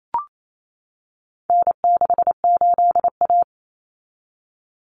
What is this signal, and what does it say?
Morse code, heard as a keyed tone, about 28 wpm, 700 Hz. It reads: N68A
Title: Common Cut Numbers - 4-Digit 28wpm